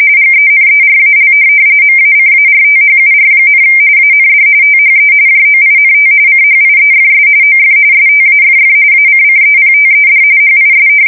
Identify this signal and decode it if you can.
RTTY, about 45.45 baud. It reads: GENERAL MACARTHUR, AND THE JOINT CHIEFS OF STAFF ACCEPTED HIS
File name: fsk170 / USS-New-Jersey